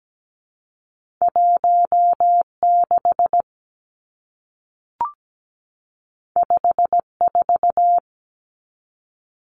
Morse code, heard as a keyed tone, about 17 wpm, 700 Hz. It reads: 16 54
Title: Numbers - 2-Digit 17wpm